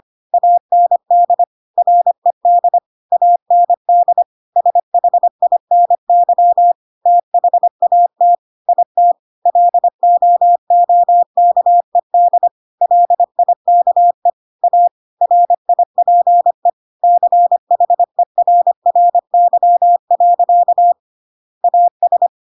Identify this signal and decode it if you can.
Morse code, heard as a keyed tone, about 25 wpm, 700 Hz. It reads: AND RED AND SHINY THAT IT LOOKED LIKE A RIPE CHERRY. AS
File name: Pinocchio / Pinocchio ch1 25 25 ik1hge